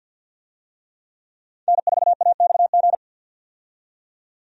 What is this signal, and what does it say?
Morse code, heard as a keyed tone, about 48 wpm, 700 Hz. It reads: N4AXG
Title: USA Call Signs - 5 Characters - 50wpm